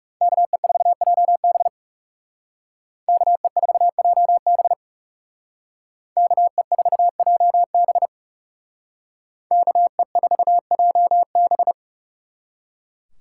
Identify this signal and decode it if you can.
Morse code, heard as a keyed tone, about 38 wpm, 700 Hz. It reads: KE4JB KE4JB KE4JB KE4JB